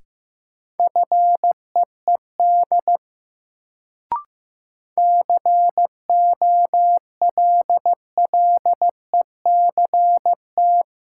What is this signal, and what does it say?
Morse code, heard as a keyed tone, about 15 wpm, 700 Hz. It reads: FEED COLLECT
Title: Sentences from Top 1600 Words - New Word Drill - 15wpm